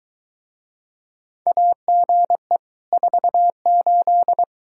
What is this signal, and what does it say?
Morse code, heard as a keyed tone, about 23 wpm, 700 Hz. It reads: AGE 48